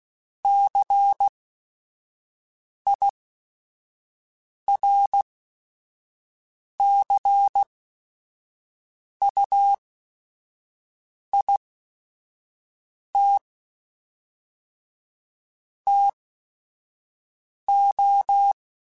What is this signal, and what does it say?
Morse code, heard as a keyed tone, about 16 wpm, spaced as for 5 wpm, 800 Hz. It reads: CIRCUIT TO
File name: Technical Subjects / Technical Subjects-mp3s / XC @5